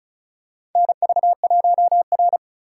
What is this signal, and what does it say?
Morse code, heard as a keyed tone, about 36 wpm, 700 Hz. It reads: NV1R
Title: USA Call Signs - 4 Characters - 2x Character Spacing - 35wpm